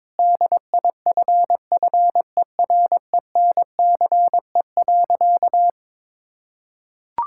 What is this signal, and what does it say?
Morse code, heard as a keyed tone, about 22 wpm, 700 Hz. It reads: DIFFERENCE.
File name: practice / 22wpm / Sentences with 3-Word Phrases - 22wpm